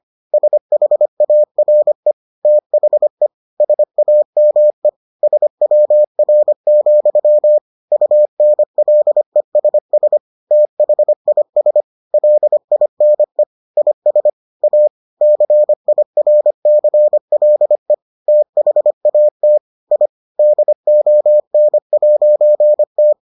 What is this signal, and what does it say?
Morse code, heard as a keyed tone, about 25 wpm, 600 Hz. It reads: SHARE THE SAME SWR, UNLESS THIS LINE IS A CIRCLE THAT I DON'T